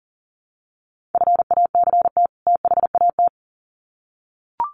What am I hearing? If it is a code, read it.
Morse code, heard as a keyed tone, about 40 wpm, 700 Hz. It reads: FACT THAT
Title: Sets of 2 Words 40wpm